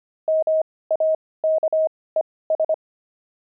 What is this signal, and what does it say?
Morse code, heard as a keyed tone, about 25 wpm, spaced as for 18 wpm, 630 Hz. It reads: MAKES
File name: pangram-fs-18-cs-25-oxide